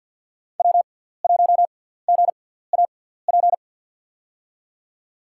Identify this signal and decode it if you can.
Morse code, heard as a keyed tone, about 53 wpm, spaced as for 17 wpm, 690 Hz. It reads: W1GAP
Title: USA Call Signs - 5 Characters - 6x Character Spacing - 50wpm